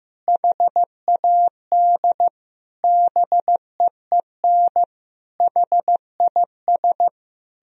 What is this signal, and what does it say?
Morse code, heard as a keyed tone, about 15 wpm, 700 Hz. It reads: HAD BEEN HIS
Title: Sentences from Top 1000 Words - Review Part 2 - 15wpm